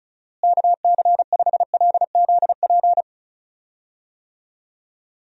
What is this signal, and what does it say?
Morse code, heard as a keyed tone, about 34 wpm, 700 Hz. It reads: KC5LZP
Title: USA Call Signs - 6 Characters - Repeated 2x - 35wpm